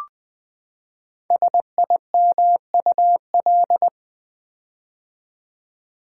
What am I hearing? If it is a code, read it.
Morse code, heard as a keyed tone, about 20 wpm, 700 Hz. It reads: SIMUL
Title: Build A Word - 5 Letters At A Time - 10-Letter Words - 20wpm